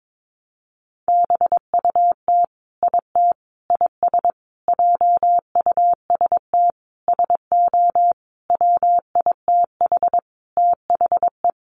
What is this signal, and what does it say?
Morse code, heard as a keyed tone, about 22 wpm, 700 Hz. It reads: BUT IT IS JUST SO WITH THE
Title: Sentences from Top 200 Words 22wpm